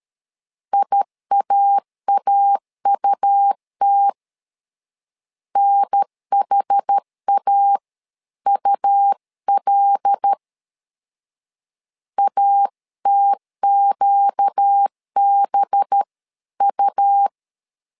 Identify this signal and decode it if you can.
Morse code, heard as a keyed tone, about 13 wpm, 800 Hz. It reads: IAAUT NHAUL ATQBU
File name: LEZIONI / 5 LBVC / 5 LB 12wpm 1906